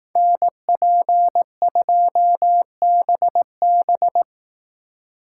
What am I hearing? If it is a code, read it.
Morse code, heard as a keyed tone, about 18 wpm, 700 Hz. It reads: NP2BB
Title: Speed Racing - USA 2x2 Call Signs 15wpm